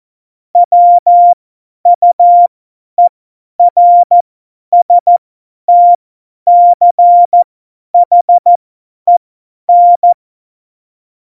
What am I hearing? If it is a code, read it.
Morse code, heard as a keyed tone, about 14 wpm, spaced as for 10 wpm, 700 Hz. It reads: WUERSTCHEN
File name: Morsekurs / Witze / Witz 02. Juli 2019, 50 F70